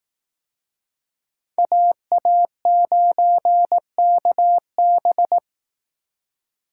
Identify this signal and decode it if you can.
Morse code, heard as a keyed tone, about 18 wpm, 700 Hz. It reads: AA9KB
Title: Speed Racing - USA Call Signs 15wpm